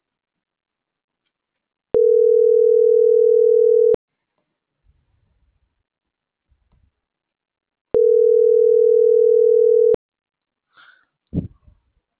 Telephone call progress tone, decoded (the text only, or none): ringback tone